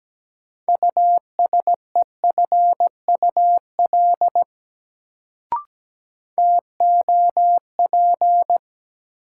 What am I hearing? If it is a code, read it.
Morse code, heard as a keyed tone, about 17 wpm, 700 Hz. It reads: USEFUL TOP
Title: Sentences from Top 700 Words - New Word Drill - 17wpm